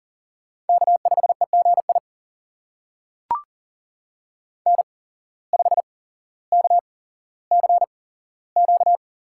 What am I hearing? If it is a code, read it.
Morse code, heard as a keyed tone, about 40 wpm, spaced as for 17 wpm, 700 Hz. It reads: K5EGI N5KCQ
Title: USA Call Signs - 5 Characters - 8x Character Spacing - 40wpm